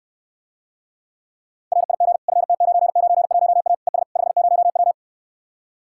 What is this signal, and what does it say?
Morse code, heard as a keyed tone, about 69 wpm, 700 Hz. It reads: RIG FT991A ES 50W